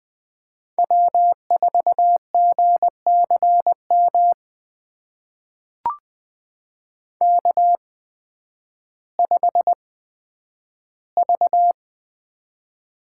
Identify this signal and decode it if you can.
Morse code, heard as a keyed tone, about 20 wpm, spaced as for 11 wpm, 700 Hz. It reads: W4GCM K5V
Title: USA Call Signs - 5 Characters - 8x Character Spacing - 20wpm